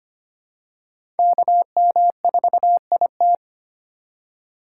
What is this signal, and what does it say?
Morse code, heard as a keyed tone, about 25 wpm, 700 Hz. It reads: KM4IT
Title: USA Call Signs - 5 Characters - 4x Character Spacing - 25wpm